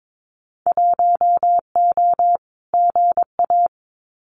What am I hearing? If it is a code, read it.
Morse code, heard as a keyed tone, about 22 wpm, 700 Hz. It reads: 1O GA
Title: Winter Field Day Exchanges 22wpm